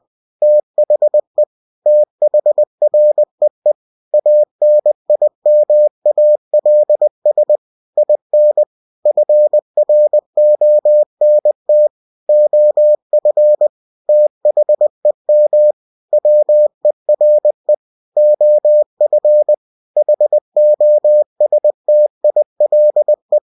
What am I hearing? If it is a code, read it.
Morse code, heard as a keyed tone, about 20 wpm, 600 Hz. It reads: THE THREE ANIMALS IN FRONT OF THEM WERE OF HOSTILE
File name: cw-books / winnie-the-pooh / winnie-20wpm-page-0023